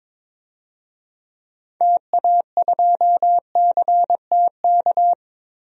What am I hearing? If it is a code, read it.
Morse code, heard as a keyed tone, about 22 wpm, 700 Hz. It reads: TA2CTK
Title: DXCC-100 Call Signs - 6 Characters - Repeated 3x - 22wpm